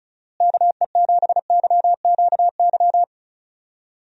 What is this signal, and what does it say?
Morse code, heard as a keyed tone, about 36 wpm, 700 Hz. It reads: KE7YQY
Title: USA Call Signs - 6 Characters - Repeated 3x - 35wpm